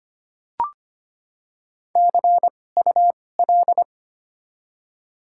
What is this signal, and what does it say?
Morse code, heard as a keyed tone, about 25 wpm, spaced as for 18 wpm, 700 Hz. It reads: CUL
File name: Top 90 QSO Elements - 2x Character Spacing - 25wpm